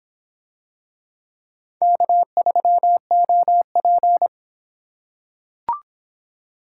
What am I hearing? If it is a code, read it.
Morse code, heard as a keyed tone, about 26 wpm, 700 Hz. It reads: K3OP